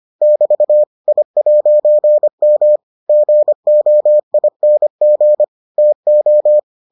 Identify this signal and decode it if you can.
Morse code, heard as a keyed tone, about 25 wpm, 600 Hz. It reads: <BT> I'M GOING TO